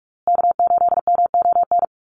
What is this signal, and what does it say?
Morse code, heard as a keyed tone, about 44 wpm, 700 Hz. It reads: K8MON
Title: USA Call Signs Repeated 2x 45wpm